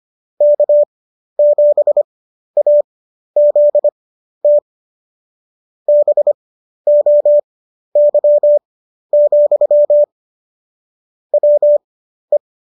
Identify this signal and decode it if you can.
Morse code, heard as a keyed tone, about 25 wpm, spaced as for 12 wpm, 590 Hz. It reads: K7AZT BOY, WE